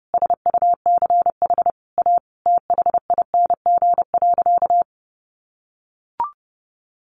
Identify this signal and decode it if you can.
Morse code, heard as a keyed tone, about 30 wpm, 700 Hz. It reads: SUCH A THING.
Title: Sentences from Top 200 Words 30wpm